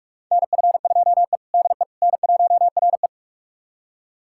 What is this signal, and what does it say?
Morse code, heard as a keyed tone, about 46 wpm, 700 Hz. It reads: NF2E DE N1RE
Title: USA Callsign DE Callsign - 4 Characters - 45wpm